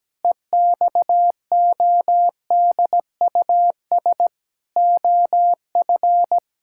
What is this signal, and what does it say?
Morse code, heard as a keyed tone, about 17 wpm, 700 Hz. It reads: EXODUS OF
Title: Pangram Challenge v2 - 17wpm